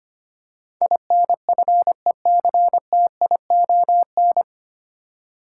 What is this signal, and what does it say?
Morse code, heard as a keyed tone, about 25 wpm, 690 Hz. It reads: INFECTION
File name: practice / 25wpm / 9 Letter Words - First 3 Letters Given - 25wpm